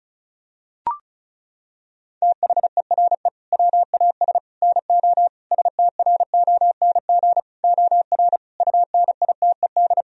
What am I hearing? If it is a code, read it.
Morse code, heard as a keyed tone, about 34 wpm, 700 Hz. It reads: THERE WAS NO STRONG OR UNITED
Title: Sentences from Top 1000 Words - Review Part 4 - 35wpm